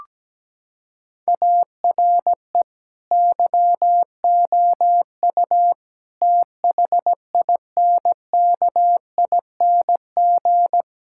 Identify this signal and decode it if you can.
Morse code, heard as a keyed tone, about 17 wpm, 700 Hz. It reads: ARE YOU THINKING
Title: Sentences from Top 1000 Words - Review Part 6 - 17wpm